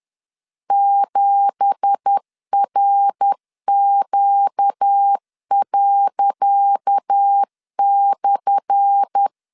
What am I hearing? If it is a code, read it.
Morse code, heard as a keyed tone, about 10 wpm, 800 Hz. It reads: 7RQ./
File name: LEZIONI / 10 SIGNS / 10C DOT 10wpm 2940